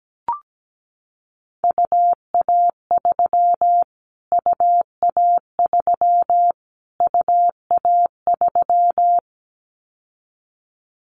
Morse code, heard as a keyed tone, about 17 wpm, 700 Hz. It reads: UA3 UA3 UA3
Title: Callsign Prefixes for Top 25 DXCC - Repeated 3x - 17wpm